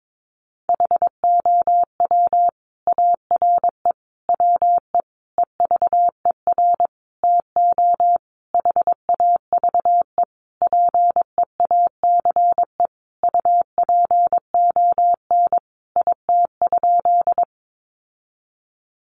Morse code, heard as a keyed tone, about 22 wpm, 700 Hz. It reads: HOW ARE WE EVER TO HAVE PEACE UPON IT?
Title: Sentences from Top 1500 Words 22wpm